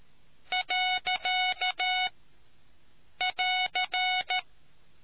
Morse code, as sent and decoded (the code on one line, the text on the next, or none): .-.-.- / .-.-.
. <AR>